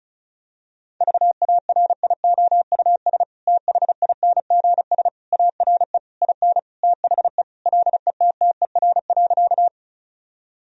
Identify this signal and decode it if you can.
Morse code, heard as a keyed tone, about 34 wpm, 700 Hz. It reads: VARIOUS THINGS ARE IN THE LETTER.
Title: Sentences from Top 1000 Words - Review Part 4 - 35wpm